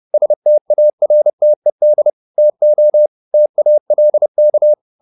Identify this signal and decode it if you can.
Morse code, heard as a keyed tone, about 30 wpm, 600 Hz. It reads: STARTED TO TALK
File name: the old man and the sea / the old man and the sea 1 30wpm